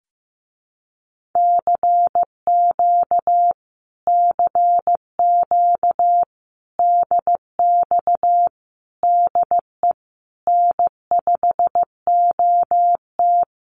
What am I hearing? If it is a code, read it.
Morse code, heard as a keyed tone, about 15 wpm, 700 Hz. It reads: CQ CQ DX DE N5OT
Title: Common QSO Phrases Encore 15wpm